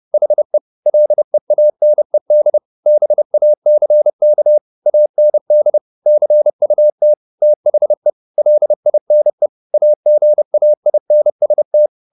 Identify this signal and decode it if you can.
Morse code, heard as a keyed tone, about 30 wpm, 600 Hz. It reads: HE LEANED BACK AND CUT THE LINE AGAINST
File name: the old man and the sea / the old man and the sea 1 30wpm